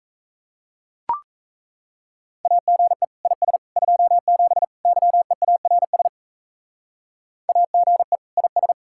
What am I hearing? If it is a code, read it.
Morse code, heard as a keyed tone, about 35 wpm, 700 Hz. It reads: AGE IS 27 YEARS AGE IS
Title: Speed-Racing Common QSO Phrases - 28wpm